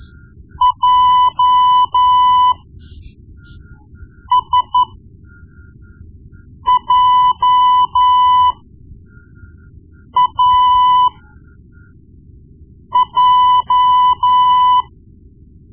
Morse code, heard as a keyed tone, about 12 wpm, spaced as for 4 wpm, 1000 Hz. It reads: JSJAJ
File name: Morse Lessons / Morse Lessons / Tape-1B